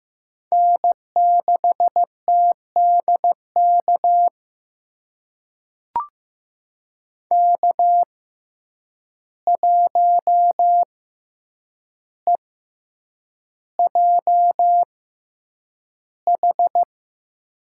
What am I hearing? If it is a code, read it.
Morse code, heard as a keyed tone, about 15 wpm, spaced as for 8 wpm, 700 Hz. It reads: N6TDK K1EJH
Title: USA Call Signs - 5 Characters - 6x Character Spacing - 15wpm